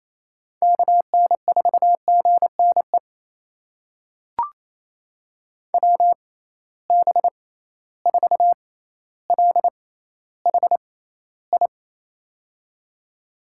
Morse code, heard as a keyed tone, about 28 wpm, spaced as for 14 wpm, 700 Hz. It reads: KN4GNE WB4LHI
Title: USA Call Signs - 6 Characters - 6x Character Spacing - 28wpm